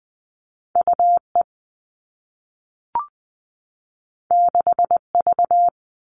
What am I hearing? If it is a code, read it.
Morse code, heard as a keyed tone, about 20 wpm, 700 Hz. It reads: UE 6V
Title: All Cut Numbers - 2-Digit 20wpm